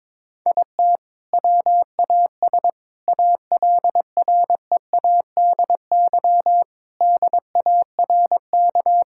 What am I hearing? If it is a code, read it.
Morse code, heard as a keyed tone, about 22 wpm, 700 Hz. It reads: IT WAS ALREADY DARK
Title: Sentences from Top 1000 Words - Review Part 5 - 22wpm